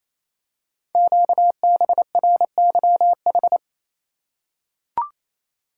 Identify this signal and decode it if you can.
Morse code, heard as a keyed tone, about 27 wpm, 700 Hz. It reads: QBRYH
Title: Code Groups - 5 Letters - 28wpm